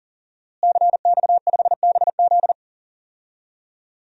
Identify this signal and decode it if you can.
Morse code, heard as a keyed tone, about 39 wpm, 700 Hz. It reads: CX5BZ
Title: DXCC-100 Call Signs - 5 Characters - Repeated 2x - 40wpm